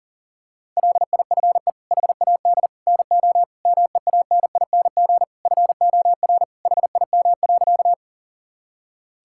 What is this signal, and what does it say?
Morse code, heard as a keyed tone, about 38 wpm, 700 Hz. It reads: LIFE HAD NO MEANING FOR HIM.
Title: Sentences from Top 1000 Words - Review Part 6 - 40wpm